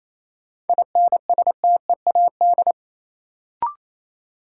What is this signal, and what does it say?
Morse code, heard as a keyed tone, about 28 wpm, 700 Hz. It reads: INSTEAD
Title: Sentences from Top 800 Words - New Word Drill - 28wpm